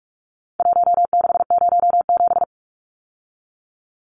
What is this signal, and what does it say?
Morse code, heard as a keyed tone, about 44 wpm, 700 Hz. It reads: 1607
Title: Numbers - Contest Style - 4-Digit 45wpm